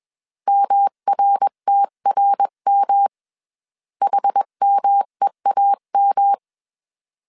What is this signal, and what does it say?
Morse code, heard as a keyed tone, about 21 wpm, 800 Hz. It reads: MRTRM HMEAM
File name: 20wpm